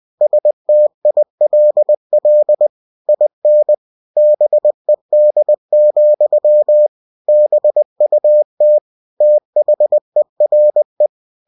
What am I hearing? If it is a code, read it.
Morse code, heard as a keyed tone, about 20 wpm, 600 Hz. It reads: STILL IN BED, BUT THERE
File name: cw-books / winnie-the-pooh / winnie-20wpm-page-0040